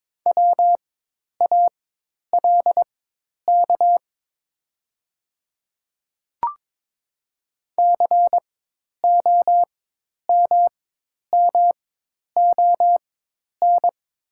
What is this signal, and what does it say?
Morse code, heard as a keyed tone, about 22 wpm, spaced as for 10 wpm, 700 Hz. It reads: WALK COMMON
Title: 4x Character Spacing - Top 500 Words - 22wpm